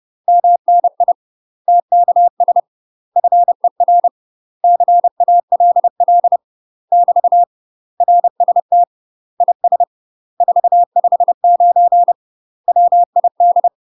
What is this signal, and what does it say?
Morse code, heard as a keyed tone, about 30 wpm, 700 Hz. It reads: MNI TKS FER CALL <BT> RST IS 459 WID
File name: UK MASTER QSOs TRAINING / 30WPM / UK MASTER QSOs 0045